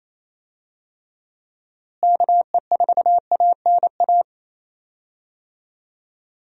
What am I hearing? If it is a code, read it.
Morse code, heard as a keyed tone, about 28 wpm, 700 Hz. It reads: KE4ANA
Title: USA Call Signs - 6 Characters - Repeated 3x - 28wpm